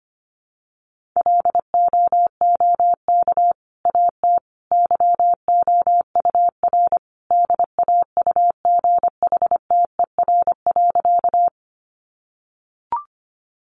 Morse code, heard as a keyed tone, about 25 wpm, 700 Hz. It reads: LOOK AT YOUR DAUGHTER.
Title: Sentences from Top 1500 Words 25wpm